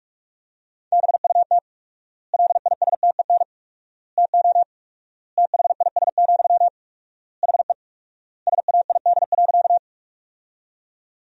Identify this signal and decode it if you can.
Morse code, heard as a keyed tone, about 46 wpm, 700 Hz. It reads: BUT LISTEN TO THIS, HE SAID.
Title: 4x Word-Spacing - Sentences from Top 400 Words - 45wpm